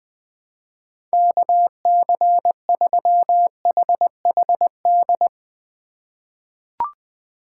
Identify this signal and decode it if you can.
Morse code, heard as a keyed tone, about 21 wpm, 700 Hz. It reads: KC3HHD
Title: USA Call Signs - 6 Characters - 3x Character Spacing - 20wpm